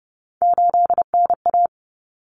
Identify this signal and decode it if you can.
Morse code, heard as a keyed tone, about 30 wpm, 700 Hz. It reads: 8NA
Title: Common Cut Numbers - 3-Digit 30wpm